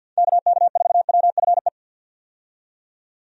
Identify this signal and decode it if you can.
Morse code, heard as a keyed tone, about 50 wpm, 700 Hz. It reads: KK4WFE
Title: USA Call Signs - 6 Characters - 50wpm